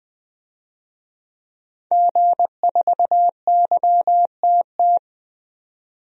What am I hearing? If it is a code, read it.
Morse code, heard as a keyed tone, about 20 wpm, 700 Hz. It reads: G4YTT
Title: DXCC-100 Call Signs - 5 Characters - 20wpm